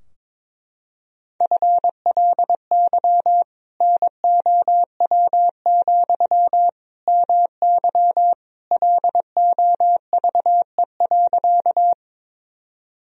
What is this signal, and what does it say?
Morse code, heard as a keyed tone, about 22 wpm, 700 Hz. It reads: FLY NOW, MY LOVE.
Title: Sentences from Top 1000 Words - Review Part 2 - 22wpm